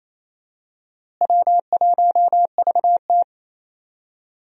Morse code, heard as a keyed tone, about 28 wpm, 700 Hz. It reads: W1VT
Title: USA Call Signs - 4 Characters - Repeated 3x - 28wpm